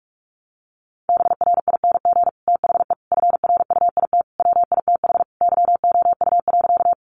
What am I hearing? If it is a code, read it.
Morse code, heard as a keyed tone, about 44 wpm, 700 Hz. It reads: BRING THE FRUIT WITH YOU.
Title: Sentences from Top 1300 Words 45wpm